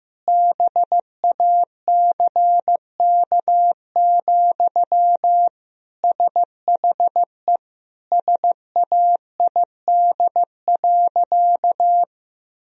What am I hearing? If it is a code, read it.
Morse code, heard as a keyed tone, about 15 wpm, 700 Hz. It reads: BACK, SHE SAID.
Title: Sentences from Top 1000 Words - Review Part 2 - 15wpm